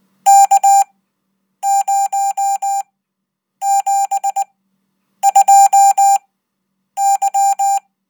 Morse code, heard as a keyed tone, about 20 wpm, spaced as for 8 wpm, 790 Hz. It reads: K072Y
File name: Code